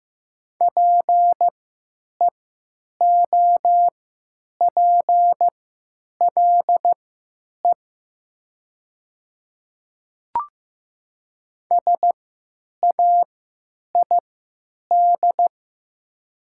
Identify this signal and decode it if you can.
Morse code, heard as a keyed tone, about 15 wpm, spaced as for 9 wpm, 700 Hz. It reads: PEOPLE SAID
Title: 3x Character Spacing - Top 100 Words - 15wpm